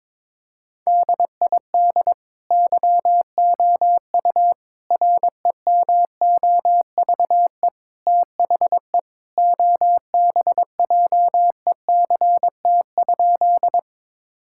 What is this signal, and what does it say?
Morse code, heard as a keyed tone, about 22 wpm, 700 Hz. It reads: DID YOU REMOVE THE OBJECT?
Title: Sentences from Top 1000 Words - Review Part 5 - 22wpm